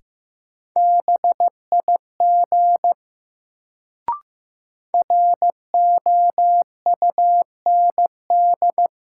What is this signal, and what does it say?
Morse code, heard as a keyed tone, about 15 wpm, 700 Hz. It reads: BIG ROUND
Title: Sentences from Top 200 Words - New Word Drill - 15wpm